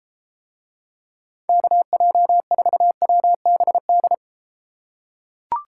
Morse code, heard as a keyed tone, about 33 wpm, 700 Hz. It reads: KJ4WBD